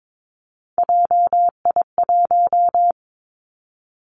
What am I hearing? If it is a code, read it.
Morse code, heard as a keyed tone, about 22 wpm, 700 Hz. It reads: JI1
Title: Three Challenging Letter-Number Combinations 22wpm